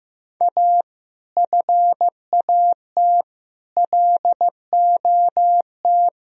Morse code, heard as a keyed tone, about 15 wpm, 700 Hz. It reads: A FAT LOT
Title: Sentences from Top 1000 Words - Review Part 4 - 15wpm